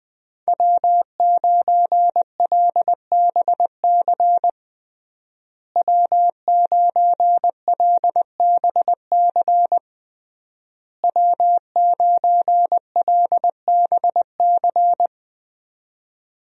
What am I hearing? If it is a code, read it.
Morse code, heard as a keyed tone, about 20 wpm, 700 Hz. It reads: W9LBC W9LBC W9LBC
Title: USA Call Signs - 5 Characters - Repeated 3x - 20wpm